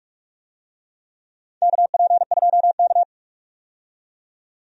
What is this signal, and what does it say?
Morse code, heard as a keyed tone, about 45 wpm, 700 Hz. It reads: KP2K
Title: DXCC-100 Call Signs - 4 Characters - 45wpm